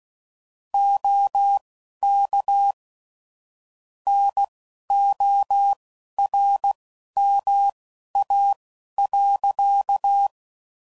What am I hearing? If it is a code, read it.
Morse code, heard as a keyed tone, about 16 wpm, spaced as for 11 wpm, 800 Hz. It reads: OK NORMA.
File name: CW Code Groups @15WPM / QSOs / WD0FBT Spaced